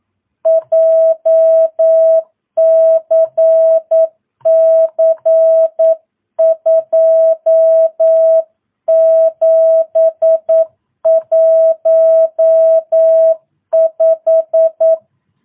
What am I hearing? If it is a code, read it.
Morse code, heard as a keyed tone, about 9 wpm, 650 Hz. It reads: JCC 2715
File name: jh8cbh / 2022cw